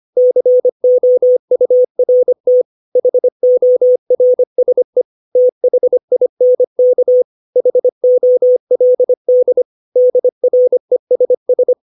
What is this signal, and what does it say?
Morse code, heard as a keyed tone, about 25 wpm, 500 Hz. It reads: COURT HORSE THINK HOLD DRESS
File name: 632 words / 632 Words 25WPM